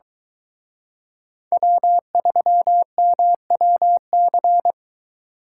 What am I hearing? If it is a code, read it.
Morse code, heard as a keyed tone, about 23 wpm, 700 Hz. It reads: W3MWC